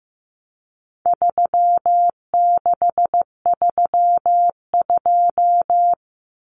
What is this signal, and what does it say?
Morse code, heard as a keyed tone, about 15 wpm, 700 Hz. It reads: 3632
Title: Common Cut Numbers - 4-Digit 15wpm